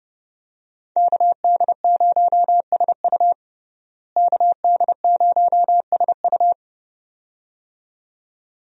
Morse code, heard as a keyed tone, about 30 wpm, 700 Hz. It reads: KD0SU KD0SU
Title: USA Call Signs - 5 Characters - Repeated 2x - 30wpm